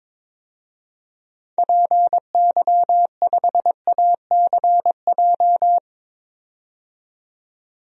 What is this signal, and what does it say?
Morse code, heard as a keyed tone, about 22 wpm, 700 Hz. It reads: PY5ACJ
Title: DXCC-100 Call Signs - 6 Characters - 22wpm